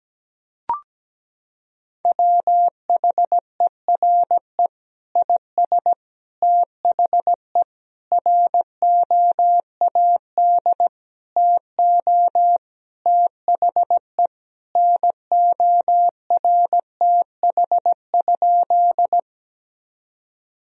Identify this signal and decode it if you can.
Morse code, heard as a keyed tone, about 17 wpm, 700 Hz. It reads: WHERE IS THE ROAD TO THE NORTH?
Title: Sentences from Top 1000 Words - Review Part 2 - 17wpm